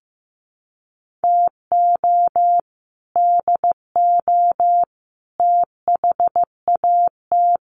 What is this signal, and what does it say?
Morse code, heard as a keyed tone, about 15 wpm, 700 Hz. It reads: TO DO THAT
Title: Sets of 3 Words 15wpm